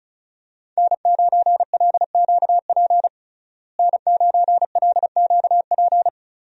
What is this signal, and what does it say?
Morse code, heard as a keyed tone, about 34 wpm, 700 Hz. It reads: N9LQP N9LQP
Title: USA Call Signs - 5 Characters - Repeated 3x - 35wpm